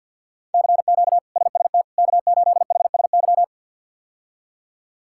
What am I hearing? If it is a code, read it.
Morse code, heard as a keyed tone, about 50 wpm, 700 Hz. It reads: CQ SST K8HSY
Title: K1USN SST QSO Practice - 50wpm